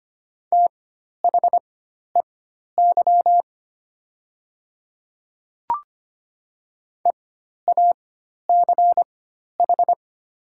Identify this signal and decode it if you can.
Morse code, heard as a keyed tone, about 25 wpm, spaced as for 12 wpm, 700 Hz. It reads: THEY EACH
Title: 4x Character Spacing - Top 200 Words - 25wpm